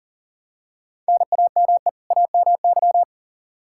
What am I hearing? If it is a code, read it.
Morse code, heard as a keyed tone, about 40 wpm, 700 Hz. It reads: NAME AMY
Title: Practice QSO - Exchange 1 - Beginning - 40wpm